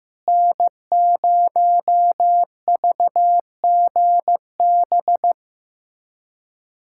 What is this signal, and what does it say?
Morse code, heard as a keyed tone, about 15 wpm, 700 Hz. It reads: N0VGB
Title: USA Callsign DE Callsign 15wpm